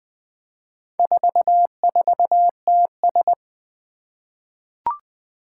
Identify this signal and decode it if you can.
Morse code, heard as a keyed tone, about 20 wpm, 700 Hz. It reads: 44TS
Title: Four Challenging Letter-Number Combinations 20wpm